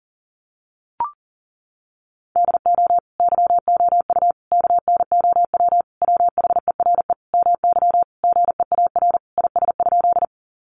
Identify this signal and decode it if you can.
Morse code, heard as a keyed tone, about 40 wpm, 700 Hz. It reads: DO YOU KNOW WHERE MY GEAR IS?
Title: Sentences from Top 1400 Words 40wpm